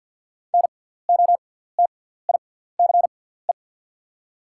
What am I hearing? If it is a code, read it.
Morse code, heard as a keyed tone, about 48 wpm, spaced as for 17 wpm, 690 Hz. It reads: NOTICE
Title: 6x Character Spacing - Top 500 Words - 50wpm